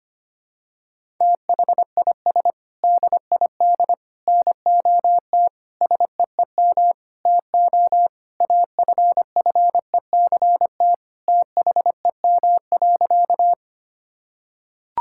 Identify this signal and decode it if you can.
Morse code, heard as a keyed tone, about 25 wpm, 700 Hz. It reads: THIS DID NOT SEEM TO AFFECT THEM.
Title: Sentences from Top 1000 Words - Review Part 6 - 25wpm